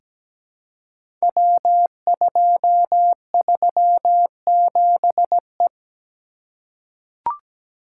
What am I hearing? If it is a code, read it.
Morse code, heard as a keyed tone, about 17 wpm, 700 Hz. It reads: W237E
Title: Code Groups - 5 Letter:Numbers - 17wpm